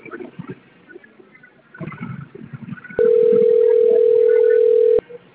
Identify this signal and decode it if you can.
Ringback tone (North America), 440 + 480 Hz.